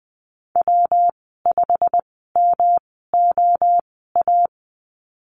Morse code, heard as a keyed tone, about 20 wpm, spaced as for 14 wpm, 700 Hz. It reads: W5MOA
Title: 219 Lesson 35g - Number 0 - Call Signs 20wpm